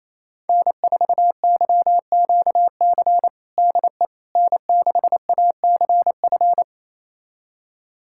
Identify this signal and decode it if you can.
Morse code, heard as a keyed tone, about 28 wpm, 700 Hz. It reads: N4YQC DE N6ACF
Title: USA Callsign DE Callsign - 5 Characters - 28wpm